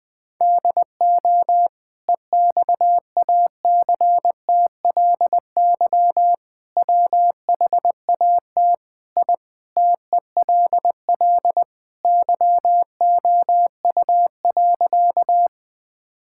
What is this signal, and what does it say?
Morse code, heard as a keyed tone, about 21 wpm, 700 Hz. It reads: DO EXACTLY WHAT I TELL YOU.
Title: Sentences from Top 1000 Words - Review Part 6 - 20wpm